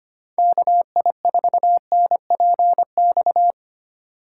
Morse code, heard as a keyed tone, about 28 wpm, 700 Hz. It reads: KI4NPX
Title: Speed Racing - USA 2x3 Call Signs 25wpm